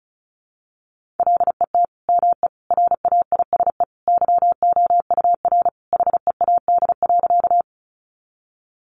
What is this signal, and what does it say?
Morse code, heard as a keyed tone, about 34 wpm, 700 Hz. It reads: LET ME RAISE YOUR HEAD.
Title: Sentences from Top 1400 Words 35wpm